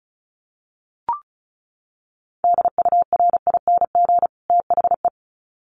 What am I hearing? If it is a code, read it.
Morse code, heard as a keyed tone, about 34 wpm, 700 Hz. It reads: DURING THE
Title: Sets of 2 Words 35wpm